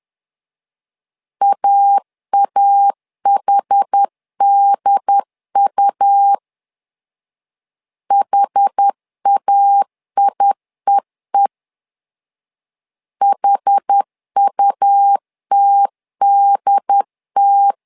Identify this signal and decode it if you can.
Morse code, heard as a keyed tone, about 10 wpm, 800 Hz. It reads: AAHDU HAIEE HUTDT
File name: LEZIONI / 2 UHDN / 2 UHD 10wpm 1826